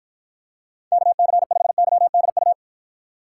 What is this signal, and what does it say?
Morse code, heard as a keyed tone, about 53 wpm, 700 Hz. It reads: KC5YDU